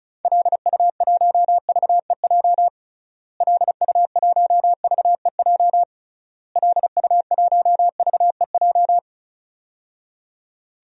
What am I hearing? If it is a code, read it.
Morse code, heard as a keyed tone, about 36 wpm, 700 Hz. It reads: LU1VEJ LU1VEJ LU1VEJ
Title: DXCC-100 Call Signs - 6 Characters - Repeated 3x - 35wpm